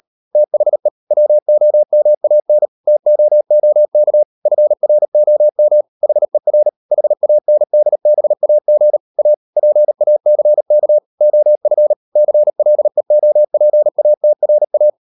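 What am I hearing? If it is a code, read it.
Morse code, heard as a keyed tone, about 38 wpm, 600 Hz. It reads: THE WOMAN TOOK FROM HER HANDBAG A PACK OF CLEOPATRA
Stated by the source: Morse code practice at 38 WPM, sending Ken Follett's novel Never in English, from CW med Gnister.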